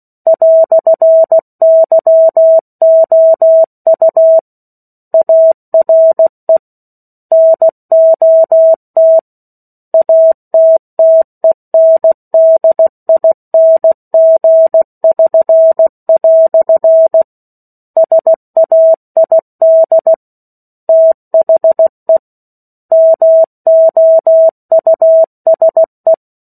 Morse code, heard as a keyed tone, about 16 wpm, 650 Hz. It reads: "YOU ARE NOT ATTENDING<SN>" SAID THE MOUSE
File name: 03-Chapter-03 16 WPM